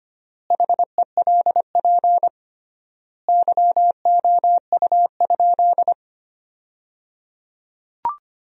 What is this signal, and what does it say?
Morse code, heard as a keyed tone, about 25 wpm, 700 Hz. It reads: HELP YOU?
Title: 3x Word-Spacing - Sentences from Top 200 Words - 25wpm